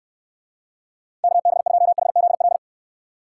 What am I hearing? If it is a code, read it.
Morse code, heard as a keyed tone, about 69 wpm, 700 Hz. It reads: KB2HZL